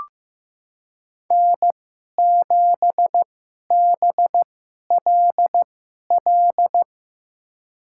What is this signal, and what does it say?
Morse code, heard as a keyed tone, about 16 wpm, spaced as for 11 wpm, 700 Hz. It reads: N7BLL
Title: USA Call Signs - 5 Characters - 2x Character Spacing - 15wpm